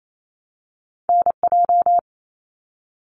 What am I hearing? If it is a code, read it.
Morse code, heard as a keyed tone, about 28 wpm, 700 Hz. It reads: NJ
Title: US States 28wpm